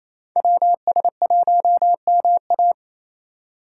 Morse code, heard as a keyed tone, about 36 wpm, 700 Hz. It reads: WS1MA